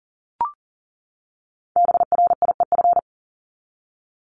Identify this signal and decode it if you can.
Morse code, heard as a keyed tone, about 40 wpm, 700 Hz. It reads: BRIEF
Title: Top 1501-1750 Words 40wpm